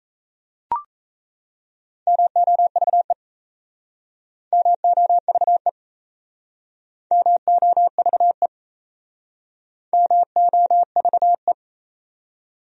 Morse code, heard as a keyed tone, about 37 wpm, 700 Hz. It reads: MOVE MOVE MOVE MOVE